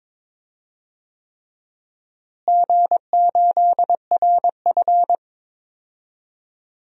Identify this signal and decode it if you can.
Morse code, heard as a keyed tone, about 22 wpm, 700 Hz. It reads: G8RF
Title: DXCC-100 Call Signs - 4 Characters - Repeated 3x - 22wpm